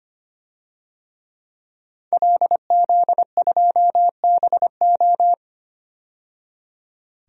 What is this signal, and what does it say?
Morse code, heard as a keyed tone, about 25 wpm, 700 Hz. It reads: LZ2BO
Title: DXCC-100 Call Signs - 5 Characters - 25wpm